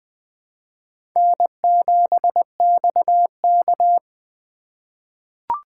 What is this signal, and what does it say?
Morse code, heard as a keyed tone, about 20 wpm, 700 Hz. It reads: N7XK
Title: USA Call Signs - 3x Character Spacing - 20wpm